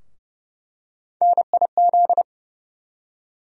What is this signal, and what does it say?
Morse code, heard as a keyed tone, about 30 wpm, 700 Hz. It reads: NIZ
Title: USA Call Sign Postfixes - 3 Letters 30wpm